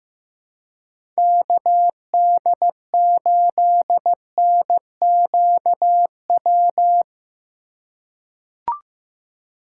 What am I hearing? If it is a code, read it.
Morse code, heard as a keyed tone, about 16 wpm, 700 Hz. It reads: KD8NQW
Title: USA Call Signs - 6 Characters - 4x Character Spacing - 15wpm